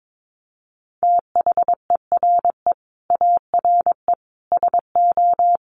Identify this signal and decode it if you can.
Morse code, heard as a keyed tone, about 22 wpm, 700 Hz. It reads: THERE ARE SO
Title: Sets of 4 Words 22wpm